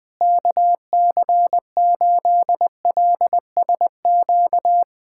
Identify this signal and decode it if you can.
Morse code, heard as a keyed tone, about 20 wpm, 700 Hz. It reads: KC8LSQ
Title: USA Callsign DE Callsign - 6 Characters - 20wpm